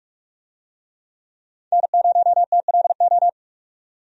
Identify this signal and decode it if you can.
Morse code, heard as a keyed tone, about 46 wpm, 700 Hz. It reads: N0TLO
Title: USA Call Signs - 5 Characters - 45wpm